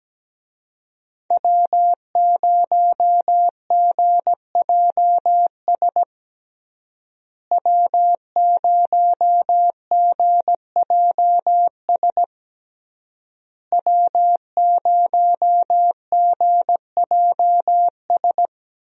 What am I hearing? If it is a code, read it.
Morse code, heard as a keyed tone, about 17 wpm, 700 Hz. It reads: W0GJS W0GJS W0GJS
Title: USA Call Signs - 5 Characters - Repeated 3x - 17wpm